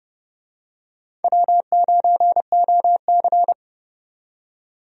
Morse code, heard as a keyed tone, about 31 wpm, 700 Hz. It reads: W9OC
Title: USA Call Signs - 4 Characters - 30wpm